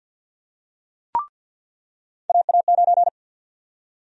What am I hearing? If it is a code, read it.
Morse code, heard as a keyed tone, about 46 wpm, 700 Hz. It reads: AA9
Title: USA Call Sign Prefixes - 50wpm